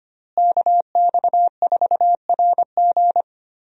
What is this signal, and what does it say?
Morse code, heard as a keyed tone, about 25 wpm, 700 Hz. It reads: KX4RG
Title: USA Call Signs - 5 Characters - Repeated 2x - 25wpm